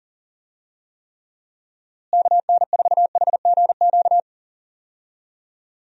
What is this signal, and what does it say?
Morse code, heard as a keyed tone, about 40 wpm, 700 Hz. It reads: KN4HGQ
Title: USA Call Signs - 6 Characters - Repeated 3x - 40wpm